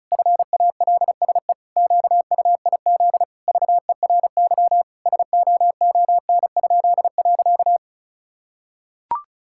Morse code, heard as a keyed tone, about 36 wpm, 700 Hz. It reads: FALSE QUIZ VERY SOON?.
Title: Pangram Challenge v2 - 35wpm